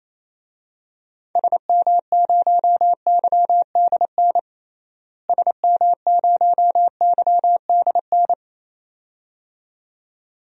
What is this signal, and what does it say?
Morse code, heard as a keyed tone, about 27 wpm, 700 Hz. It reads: SM0YDN SM0YDN
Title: DXCC-100 Call Signs - 6 Characters - Repeated 2x - 28wpm